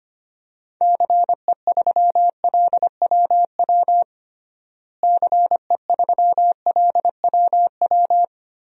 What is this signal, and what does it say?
Morse code, heard as a keyed tone, about 25 wpm, 700 Hz. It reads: CE3LWW CE3LWW
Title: DXCC-100 Call Signs - 6 Characters - Repeated 3x - 25wpm